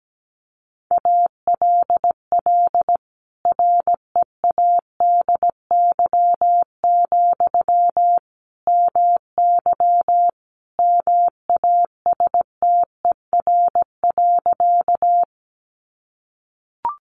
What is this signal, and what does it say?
Morse code, heard as a keyed tone, about 17 wpm, 700 Hz. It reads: ALL READY, MY MASTER.
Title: Sentences from Top 1300 Words 17wpm